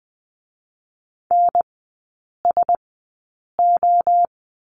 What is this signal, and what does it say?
Morse code, heard as a keyed tone, about 20 wpm, spaced as for 8 wpm, 700 Hz. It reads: NSO
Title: 035 Lesson 7e - Letter s - Three Characters 20wpm